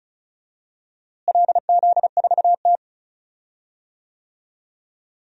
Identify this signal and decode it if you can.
Morse code, heard as a keyed tone, about 36 wpm, 700 Hz. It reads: LZ4T
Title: DXCC-100 Call Signs - 4 Characters - Repeated 3x - 35wpm